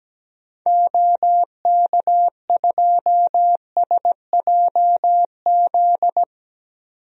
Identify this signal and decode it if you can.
Morse code, heard as a keyed tone, about 17 wpm, 700 Hz. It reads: OK2SJZ
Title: DXCC-100 Call Signs - 6 Characters - 17wpm